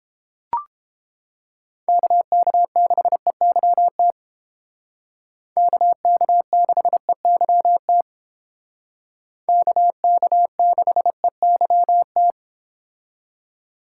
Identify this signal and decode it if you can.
Morse code, heard as a keyed tone, about 30 wpm, 700 Hz. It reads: KK6EYT KK6EYT KK6EYT